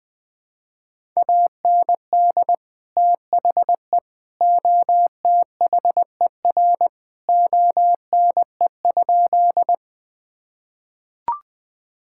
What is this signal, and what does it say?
Morse code, heard as a keyed tone, about 21 wpm, 700 Hz. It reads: AND THE OTHER ONE?
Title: Sentences with 3-Word Phrases - 20wpm